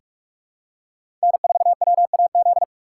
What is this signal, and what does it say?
Morse code, heard as a keyed tone, about 46 wpm, 700 Hz. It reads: N4WAZ
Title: USA Call Signs - 5 Characters - 3x Character Spacing - 45wpm